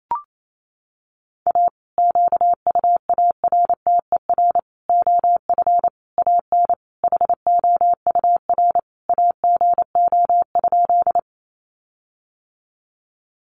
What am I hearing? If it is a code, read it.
Morse code, heard as a keyed tone, about 28 wpm, 700 Hz. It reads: A QUARTER OF AN HOUR AGO?
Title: Sentences from Top 1300 Words 28wpm